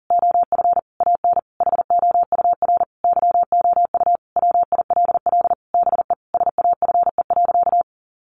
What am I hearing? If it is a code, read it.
Morse code, heard as a keyed tone, about 40 wpm, 700 Hz. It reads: OF AN HOUR YOU WILL BE SAFE.
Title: Sentences from Top 1300 Words 40wpm